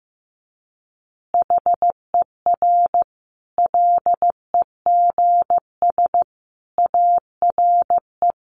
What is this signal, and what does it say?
Morse code, heard as a keyed tone, about 15 wpm, 700 Hz. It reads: HER LEGS ARE
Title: Easy Phrases 15wpm